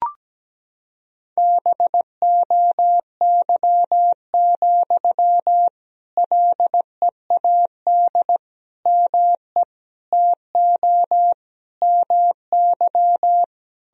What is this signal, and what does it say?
Morse code, heard as a keyed tone, about 17 wpm, 700 Hz. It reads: BOY, LEAD ME TO MY
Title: Sentences from Top 1000 Words - Review Part 2 - 17wpm